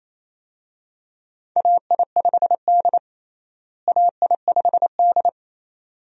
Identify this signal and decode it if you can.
Morse code, heard as a keyed tone, about 28 wpm, 700 Hz. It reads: AI5D AI5D
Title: USA Call Signs - 4 Characters - Repeated 3x - 28wpm